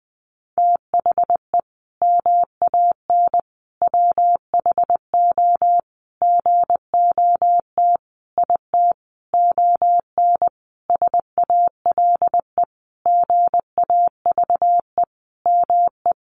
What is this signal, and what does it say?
Morse code, heard as a keyed tone, about 20 wpm, 700 Hz. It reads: THE MAN WHO GOT IT ON SALE GAVE ME
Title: Sentences from Top 1400 Words 20wpm